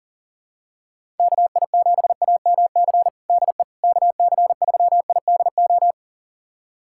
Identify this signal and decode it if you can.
Morse code, heard as a keyed tone, about 40 wpm, 700 Hz. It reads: KI7AMC DE KC3IDO
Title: USA Callsign DE Callsign 40wpm